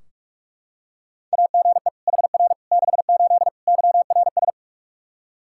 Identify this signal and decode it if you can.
Morse code, heard as a keyed tone, about 46 wpm, 700 Hz. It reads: AGE HR 68 YRS
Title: K1USN SST QSO Practice 45wpm